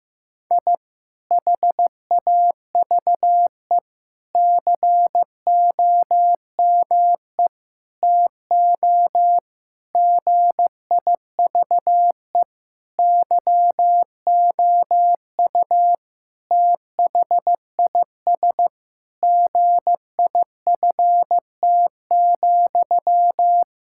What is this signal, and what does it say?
Morse code, heard as a keyed tone, about 15 wpm, 700 Hz. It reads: I HAVE COME TO GIVE YOU THIS GIFT,
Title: Sentences from Top 1000 Words - Review Part 6 - 15wpm